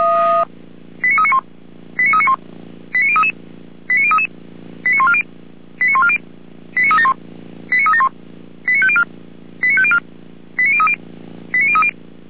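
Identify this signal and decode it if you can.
Five-tone selective call (ZVEI), heard as quick runs of five tones, tones of about 70 ms. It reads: 89391 89391 89033 89033 89140 89140 89371 89371 89504 89504 89030 89030
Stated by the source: ZVEI five-tone selective call received by SDR.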